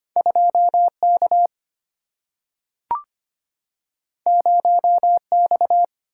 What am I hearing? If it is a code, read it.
Morse code, heard as a keyed tone, about 25 wpm, 700 Hz. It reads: 2K 0X
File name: Two Letter-Number Combinations - 25wpm